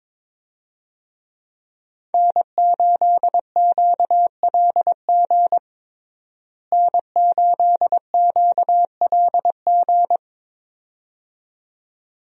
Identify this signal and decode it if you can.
Morse code, heard as a keyed tone, about 21 wpm, 700 Hz. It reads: N8QLG N8QLG
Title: USA Call Signs - 5 Characters - Repeated 2x - 22wpm